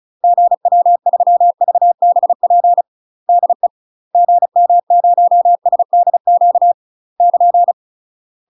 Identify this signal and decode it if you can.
Morse code, heard as a keyed tone, about 36 wpm, 700 Hz. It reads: GW3VBP DE GM0SDQ <KN>
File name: UK MASTER QSOs TRAINING / 35WPM / UK MASTER QSOs 0013